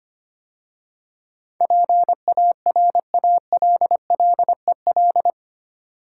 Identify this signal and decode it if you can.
Morse code, heard as a keyed tone, about 25 wpm, 700 Hz. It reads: PARALLEL
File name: 8 Letter Words - First 3 Letters Given - 25wpm